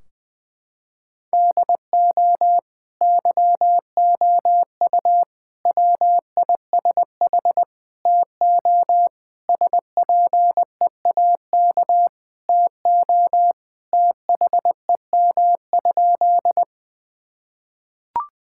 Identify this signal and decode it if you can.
Morse code, heard as a keyed tone, about 20 wpm, 700 Hz. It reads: DO YOU WISH TO SPEAK TO THEM?
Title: Sentences from Top 1600 Words 20wpm